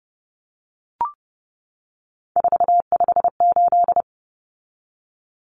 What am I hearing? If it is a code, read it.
Morse code, heard as a keyed tone, about 30 wpm, 700 Hz. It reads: 458
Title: Common Cut Numbers - 3-Digit 30wpm